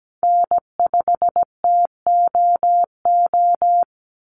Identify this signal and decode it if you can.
Morse code, heard as a keyed tone, about 17 wpm, 700 Hz. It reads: N5TOO
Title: Call Signs - 17wpm